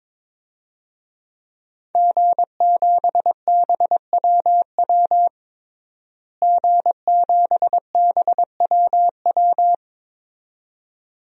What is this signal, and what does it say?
Morse code, heard as a keyed tone, about 22 wpm, 700 Hz. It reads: G7BWW G7BWW
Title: DXCC-100 Call Signs - 5 Characters - Repeated 2x - 22wpm